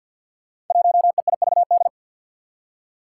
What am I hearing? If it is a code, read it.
Morse code, heard as a keyed tone, about 52 wpm, 700 Hz. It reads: 1EIVD
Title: Five Challenging Letter-Number Combinations 50wpm